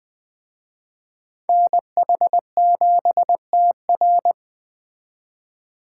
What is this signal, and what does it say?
Morse code, heard as a keyed tone, about 21 wpm, 700 Hz. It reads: NH7TR
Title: Speed Racing - USA 2x2 Call Signs 17wpm